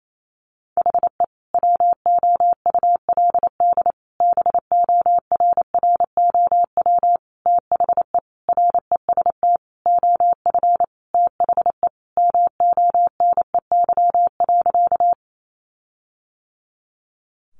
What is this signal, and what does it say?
Morse code, heard as a keyed tone, about 28 wpm, 700 Hz. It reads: HE WOULD BORROW THE REST OF THE MONEY.
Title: Sentences from Top 1500 Words 28wpm